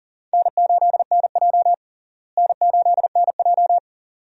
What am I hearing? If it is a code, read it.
Morse code, heard as a keyed tone, about 40 wpm, 700 Hz. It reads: N8NJ N8NJ
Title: USA Call Signs - 4 Characters - Repeated 2x - 40wpm